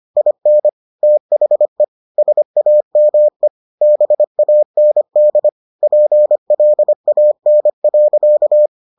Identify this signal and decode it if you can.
Morse code, heard as a keyed tone, about 25 wpm, 600 Hz. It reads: IN THE SAME BAND PLAN.